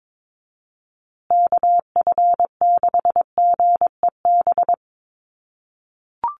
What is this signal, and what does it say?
Morse code, heard as a keyed tone, about 22 wpm, 700 Hz. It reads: KF6GEB
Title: USA Call Signs Repeated 3x 22wpm